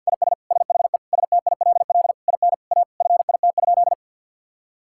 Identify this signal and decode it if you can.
Morse code, heard as a keyed tone, about 50 wpm, 700 Hz. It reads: IS SHE STILL IN A FIT?
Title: Sentences from Top 1000 Words - Review Part 5 - 50wpm